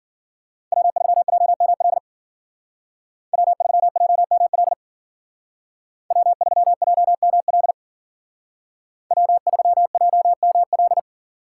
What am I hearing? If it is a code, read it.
Morse code, heard as a keyed tone, about 49 wpm, 700 Hz. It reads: W3JML W3JML W3JML W3JML
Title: Speed Racing - USA 1x3 Call Signs 40wpm